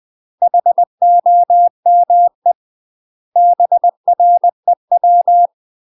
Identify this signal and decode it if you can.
Morse code, heard as a keyed tone, about 20 wpm, 700 Hz. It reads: HOME BREW
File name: UK MASTER QSOs TRAINING / 20WPM / UK MASTER QSOs 0036